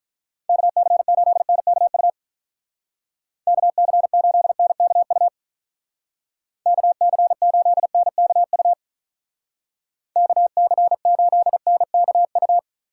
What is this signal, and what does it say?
Morse code, heard as a keyed tone, about 44 wpm, 700 Hz. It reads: KC8NKU KC8NKU KC8NKU KC8NKU